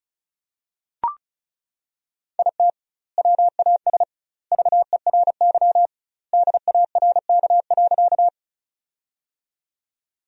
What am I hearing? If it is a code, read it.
Morse code, heard as a keyed tone, about 34 wpm, 700 Hz. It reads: IT WAS VERY DARK.
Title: 2x Word-Spacing - Sentences from Top 500 Words - 35wpm